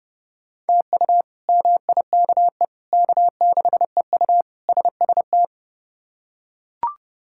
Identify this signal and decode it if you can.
Morse code, heard as a keyed tone, about 30 wpm, 700 Hz. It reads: TU MIKE K6EU SST
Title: K1USN SST QSO Practice - 30wpm